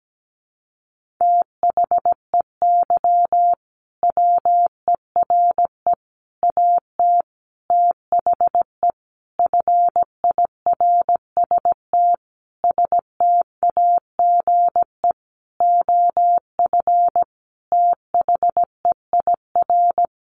Morse code, heard as a keyed tone, about 17 wpm, 700 Hz. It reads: THEY WERE AT THE FIRST STAGE OF THEIR
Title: Sentences from Top 1200 Words 17wpm